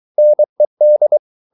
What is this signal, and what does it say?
Morse code, heard as a keyed tone, about 23 wpm, 600 Hz. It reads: NED
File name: Ferie QRS 23wpm